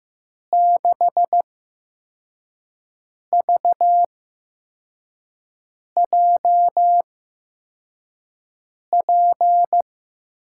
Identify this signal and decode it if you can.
Morse code, heard as a keyed tone, about 15 wpm, spaced as for 4 wpm, 700 Hz. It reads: 6VJP
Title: USA Call Signs - 6 Characters - 8x Character Spacing - 15wpm